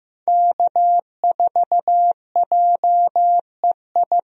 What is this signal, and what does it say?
Morse code, heard as a keyed tone, about 16 wpm, 700 Hz. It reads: K4JEI
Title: USA Call Signs - 5 Characters - Repeated 2x - 15wpm